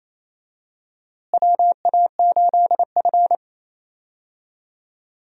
Morse code, heard as a keyed tone, about 28 wpm, 700 Hz. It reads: WA8F
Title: USA Call Signs - 4 Characters - Repeated 3x - 28wpm